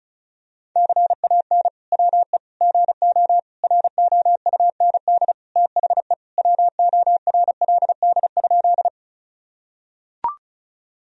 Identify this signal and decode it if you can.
Morse code, heard as a keyed tone, about 36 wpm, 700 Hz. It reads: CAN WE GO ROUND THE WORLD?
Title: Sentences from Top 1000 Words - Review Part 1 - 35wpm